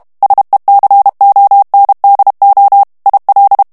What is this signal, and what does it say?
Morse code, heard as a keyed tone, about 32 wpm, 800 Hz. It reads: SECONDO IL
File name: ITALIEN texte clair 2 vitesse 30 mots minute